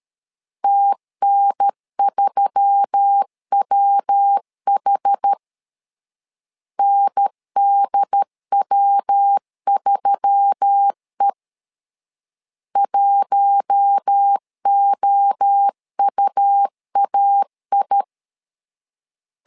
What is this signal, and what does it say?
Morse code, heard as a keyed tone, about 13 wpm, 800 Hz. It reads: TN3WH NDW3E 1OUAI
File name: LEZIONI / 4 Q1S3 12wpm 2023